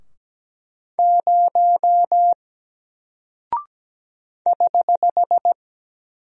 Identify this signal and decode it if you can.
Morse code, heard as a keyed tone, about 17 wpm, 700 Hz. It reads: 0 <HH>
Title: Single Character (Letter:Number:Punctuation:Prosign) Alt 17wpm